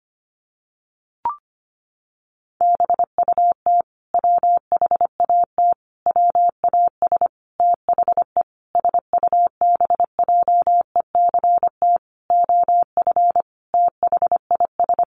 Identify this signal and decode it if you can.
Morse code, heard as a keyed tone, about 25 wpm, 700 Hz. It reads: BUT WHAT WAS THE SUBJECT OF THIS
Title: Sentences from Top 1100 Words 25wpm